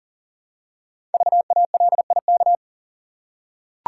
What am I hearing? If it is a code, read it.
Morse code, heard as a keyed tone, about 38 wpm, 700 Hz. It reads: VALIK
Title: Code Groups - 5 Letters - 40wpm